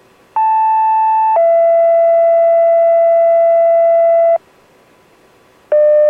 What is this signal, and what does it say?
Two-tone page, heard as a one-second tone, then a three-second tone, about 880 then 670 Hz. It is heard from 0.4 to 4.4 s.